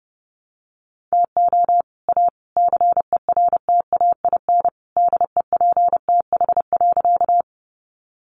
Morse code, heard as a keyed tone, about 30 wpm, 700 Hz. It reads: TO A CERTAIN DEPTH.
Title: Sentences from Top 1200 Words 30wpm